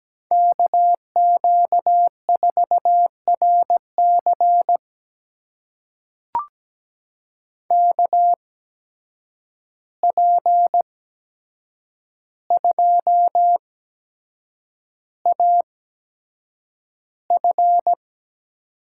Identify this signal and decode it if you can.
Morse code, heard as a keyed tone, about 17 wpm, spaced as for 7 wpm, 700 Hz. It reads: KQ4RC KP2AF
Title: USA Call Signs - 5 Characters - 8x Character Spacing - 17wpm